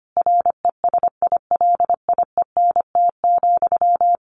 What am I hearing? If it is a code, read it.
Morse code, heard as a keyed tone, about 25 wpm, 700 Hz. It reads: RESILIENT,
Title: A Sound Of Thunder 25wpm